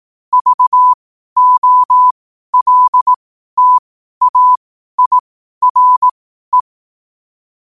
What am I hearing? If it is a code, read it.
Morse code, heard as a keyed tone, about 18 wpm, spaced as for 12 wpm, 1000 Hz. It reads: VOLTAIRE
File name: qotdfiles / 12wpm